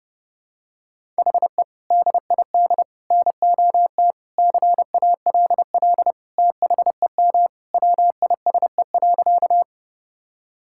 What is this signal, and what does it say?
Morse code, heard as a keyed tone, about 32 wpm, 700 Hz. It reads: HE DID NOT CALL THEM WISE.
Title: Sentences from Top 1600 Words 30wpm